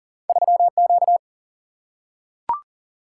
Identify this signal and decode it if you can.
Morse code, heard as a keyed tone, about 39 wpm, 700 Hz. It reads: 3Q
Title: Single-Letter Single-Number 40wpm